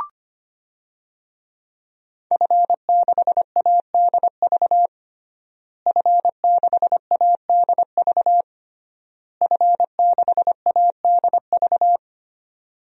Morse code, heard as a keyed tone, about 25 wpm, 700 Hz. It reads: F6ADV F6ADV F6ADV
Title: DXCC-100 Call Signs - 5 Characters - Repeated 3x - 25wpm